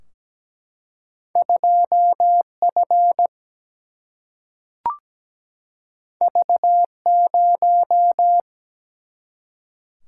Morse code, heard as a keyed tone, about 17 wpm, 700 Hz. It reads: 2F V0
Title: Single-Letter Single-Number 17wpm